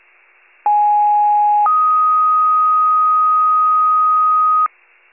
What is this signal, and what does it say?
Two-tone page, heard as a one-second tone, then a three-second tone, about 830 then 1250 Hz. It is heard from 0.6 to 4.7 s.
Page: County Wide Paging Archive